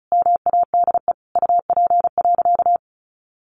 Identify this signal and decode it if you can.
Morse code, heard as a keyed tone, about 34 wpm, 700 Hz. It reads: MADE UP.
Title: Sentences from Top 1300 Words 35wpm